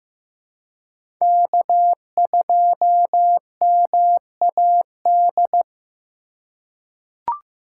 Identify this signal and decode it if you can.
Morse code, heard as a keyed tone, about 15 wpm, 700 Hz. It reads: K2MAD
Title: USA Call Signs - 5 Characters - 3x Character Spacing - 15wpm